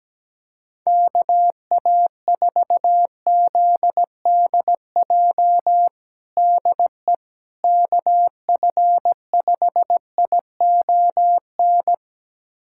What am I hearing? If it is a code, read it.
Morse code, heard as a keyed tone, about 17 wpm, 700 Hz. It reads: KA4ZDJ DE KF5ION
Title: USA Callsign DE Callsign - 6 Characters - 17wpm